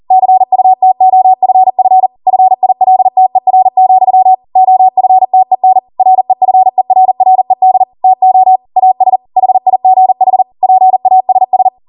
Morse code, heard as a keyed tone, about 40 wpm, 750 Hz. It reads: CUTOFF FILTER, OFTEN REFERRED TO AS HIGH PASS